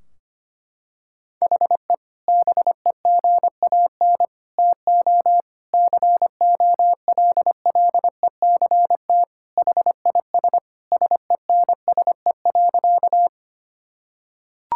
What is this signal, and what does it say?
Morse code, heard as a keyed tone, about 25 wpm, 700 Hz. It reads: HE BEGAN TO COLLECT HIS SENSE.
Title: Sentences from Top 1600 Words 25wpm